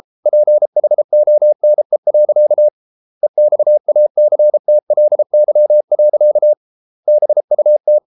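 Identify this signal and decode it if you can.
Morse code, heard as a keyed tone, about 33 wpm, 600 Hz. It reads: PHONE. EXACTLY. BUT
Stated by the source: Morse code practice at 32 wpm from CW med Gnister, reading Ken Follett's novel Never in English.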